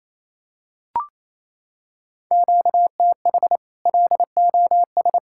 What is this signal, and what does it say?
Morse code, heard as a keyed tone, about 28 wpm, 700 Hz. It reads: QTH LOS
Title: Practice QSO - Exchange 1 - Beginning - 28wpm